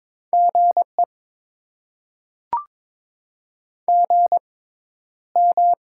Morse code, heard as a keyed tone, about 22 wpm, spaced as for 11 wpm, 700 Hz. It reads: GE GM
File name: Top 20 QSO Elements - 6x Character Spacing - 22wpm